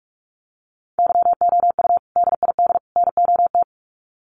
Morse code, heard as a keyed tone, about 44 wpm, 700 Hz. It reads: YOU DID NOT
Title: Sets of 3 Words 45wpm